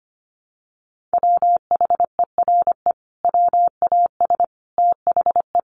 Morse code, heard as a keyed tone, about 25 wpm, 700 Hz. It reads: WHERE WAS THE
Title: Sentences from Top 1000 Words 25wpm